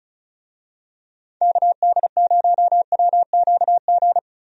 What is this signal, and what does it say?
Morse code, heard as a keyed tone, about 36 wpm, 700 Hz. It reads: KD0WQG
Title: USA Call Signs - 6 Characters - 35wpm